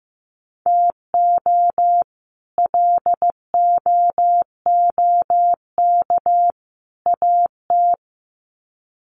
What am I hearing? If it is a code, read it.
Morse code, heard as a keyed tone, about 15 wpm, 700 Hz. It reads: TO LOOK AT
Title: Sets of 3 Words 15wpm